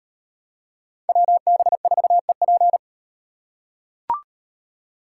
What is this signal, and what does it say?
Morse code, heard as a keyed tone, about 35 wpm, 700 Hz. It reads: WB4EP